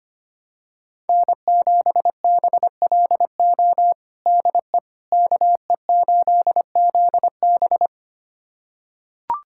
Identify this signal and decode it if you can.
Morse code, heard as a keyed tone, about 25 wpm, 700 Hz. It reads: N7BLO DE KE8ZB
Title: USA Callsign DE Callsign - 5 Characters - 25wpm